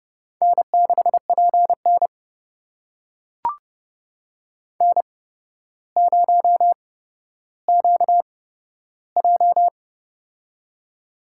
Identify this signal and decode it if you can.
Morse code, heard as a keyed tone, about 31 wpm, spaced as for 13 wpm, 700 Hz. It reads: N6PN N0QJ
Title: USA Call Signs - 4 Characters - 8x Character Spacing - 30wpm